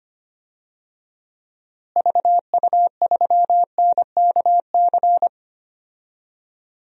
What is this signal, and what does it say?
Morse code, heard as a keyed tone, about 25 wpm, 700 Hz. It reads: VU3NKC
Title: DXCC-100 Call Signs - 6 Characters - Repeated 3x - 25wpm